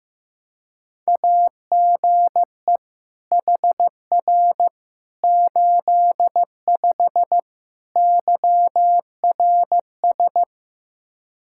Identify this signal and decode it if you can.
Morse code, heard as a keyed tone, about 15 wpm, 700 Hz. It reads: AGE HR 85 YRS
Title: Practice QSO - Exchange 3 - More Info - 15wpm